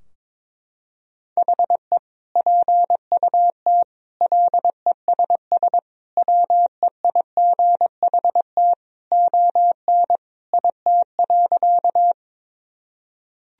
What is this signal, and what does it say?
Morse code, heard as a keyed tone, about 22 wpm, 700 Hz. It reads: HE PUT LESS WEIGHT ON IT.
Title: Sentences from Top 1000 Words - Review Part 3 - 22wpm